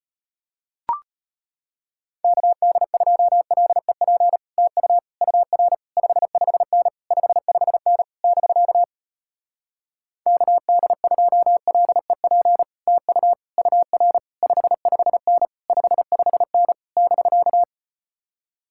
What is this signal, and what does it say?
Morse code, heard as a keyed tone, about 36 wpm, 700 Hz. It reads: KD2LEP TU UR 55N 55N <BK> KD2LEP TU UR 55N 55N <BK>